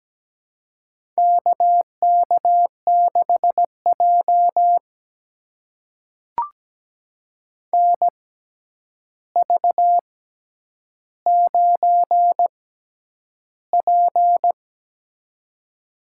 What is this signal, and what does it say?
Morse code, heard as a keyed tone, about 17 wpm, spaced as for 9 wpm, 700 Hz. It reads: KK6J NV9P
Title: USA Call Signs - 6x Character Spacing - 17wpm